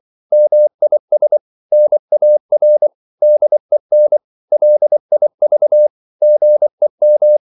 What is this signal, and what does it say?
Morse code, heard as a keyed tone, about 24 wpm, 600 Hz. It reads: MIS NAR DEN LIV GEM
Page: Korte danske ord 24wpm